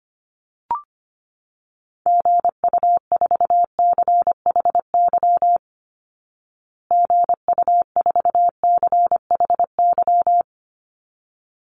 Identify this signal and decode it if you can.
Morse code, heard as a keyed tone, about 25 wpm, 700 Hz. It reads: GU4CHY GU4CHY
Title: Call Signs Repeated 3x - 25wpm